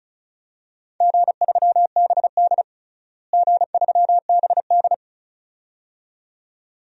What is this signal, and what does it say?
Morse code, heard as a keyed tone, about 36 wpm, 700 Hz. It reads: G3BD G3BD
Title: DXCC-100 Call Signs - 4 Characters - Repeated 2x - 35wpm